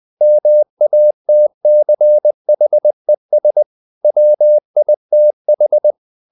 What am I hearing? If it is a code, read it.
Morse code, heard as a keyed tone, about 20 wpm, 600 Hz. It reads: MATCHES WITH
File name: the old man and the sea part 2 20wpm